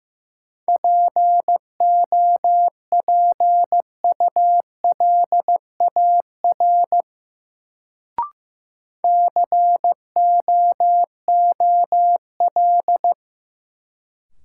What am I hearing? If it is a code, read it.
Morse code, heard as a keyed tone, about 15 wpm, 700 Hz. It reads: POPULAR COOL
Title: Sentences from Top 1200 Words - New Word Drill Alt - 15wpm